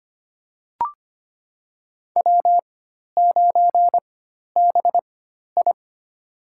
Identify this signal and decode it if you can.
Morse code, heard as a keyed tone, about 25 wpm, spaced as for 12 wpm, 700 Hz. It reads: W9BI
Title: USA Call Signs - 4x Character Spacing - 25wpm